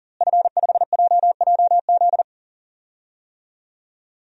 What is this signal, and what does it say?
Morse code, heard as a keyed tone, about 40 wpm, 700 Hz. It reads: F5JJZ
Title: DXCC-100 Call Signs - 5 Characters - Repeated 3x - 40wpm